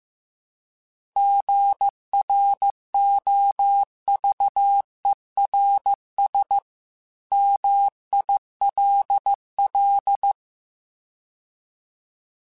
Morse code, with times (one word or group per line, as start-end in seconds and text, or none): 1.2-6.6 GROVERS
7.3-10.3 MILL